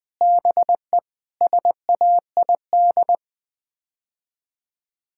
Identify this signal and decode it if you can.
Morse code, heard as a keyed tone, about 20 wpm, 700 Hz. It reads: BE SAID
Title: Sets of 6 Words - 20wpm